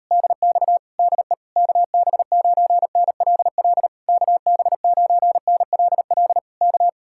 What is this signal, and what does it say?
Morse code, heard as a keyed tone, about 39 wpm, 700 Hz. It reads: DX DE KB9NLL KB9NLL K